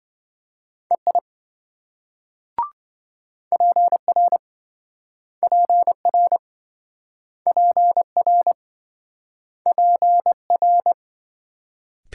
Morse code, heard as a keyed tone, about 27 wpm, 700 Hz. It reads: EI PR PR PR PR